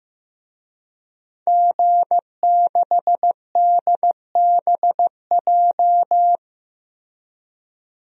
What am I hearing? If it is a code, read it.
Morse code, heard as a keyed tone, about 15 wpm, 700 Hz. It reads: G6DBJ
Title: DXCC-100 Call Signs - 5 Characters - Repeated 3x - 15wpm